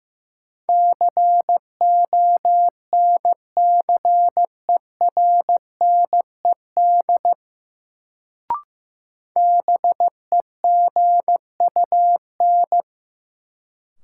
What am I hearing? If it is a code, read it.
Morse code, heard as a keyed tone, about 15 wpm, 700 Hz. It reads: CONCERNED BEGUN
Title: Sentences from Top 1000 Words - Review Part 6 - New Word Drill - 15wpm